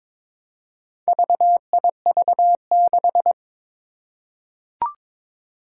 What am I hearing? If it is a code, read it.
Morse code, heard as a keyed tone, about 23 wpm, 700 Hz. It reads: VIV6
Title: Four Challenging Letter-Number Combinations 22wpm